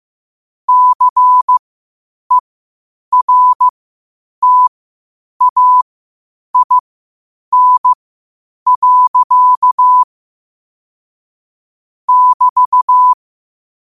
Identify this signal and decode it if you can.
Morse code, heard as a keyed tone, about 15 wpm, spaced as for 8 wpm, 1000 Hz. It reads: CERTAIN. <BT>
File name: qotdfiles / 8wpm